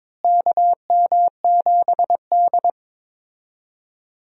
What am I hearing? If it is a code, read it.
Morse code, heard as a keyed tone, about 22 wpm, 700 Hz. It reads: KM7D
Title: USA Call Signs - 4 Characters - Repeated 2x - 22wpm